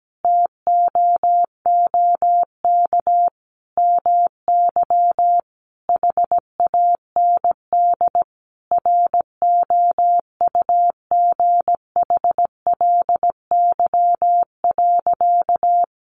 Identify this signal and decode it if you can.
Morse code, heard as a keyed tone, about 17 wpm, 700 Hz. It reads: TOOK MY HAND ROUGHLY.
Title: Sentences from Top 1500 Words 17wpm